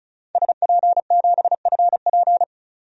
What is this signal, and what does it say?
Morse code, heard as a keyed tone, about 34 wpm, 700 Hz. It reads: SP7FP
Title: DXCC-100 Call Signs - 5 Characters - 35wpm